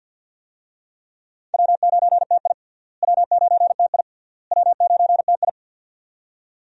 Morse code, heard as a keyed tone, about 51 wpm, 700 Hz. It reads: W9TI W9TI W9TI
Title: USA Call Signs - 4 Characters - Repeated 3x - 50wpm